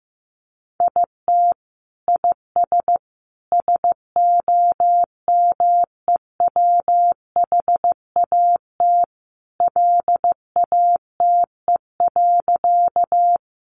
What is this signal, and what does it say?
Morse code, heard as a keyed tone, about 15 wpm, 700 Hz. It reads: IT IS SOMEWHAT LATE.
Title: Sentences from Top 1300 Words 15wpm